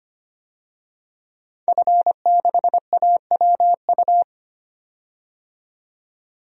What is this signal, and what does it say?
Morse code, heard as a keyed tone, about 25 wpm, 700 Hz. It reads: F6AWU
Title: DXCC-100 Call Signs - 5 Characters - 25wpm